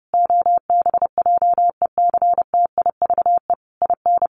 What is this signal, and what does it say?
Morse code, heard as a keyed tone, about 30 wpm, 700 Hz. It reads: OBJECTIVE IN
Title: Sentences from Top 1200 Words 30wpm